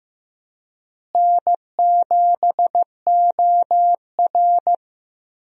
Morse code, heard as a keyed tone, about 15 wpm, 700 Hz. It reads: N7OR
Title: USA Call Signs - 4 Characters - Repeated 3x - 15wpm